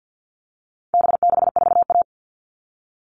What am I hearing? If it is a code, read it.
Morse code, heard as a keyed tone, about 48 wpm, 700 Hz. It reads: B64A
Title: All Cut Numbers - 4-Digit 50wpm